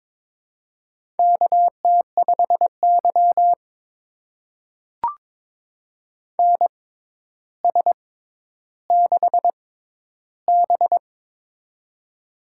Morse code, heard as a keyed tone, about 22 wpm, spaced as for 11 wpm, 700 Hz. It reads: KT5Y NS6B
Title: USA Call Signs - 4 Characters - 6x Character Spacing - 22wpm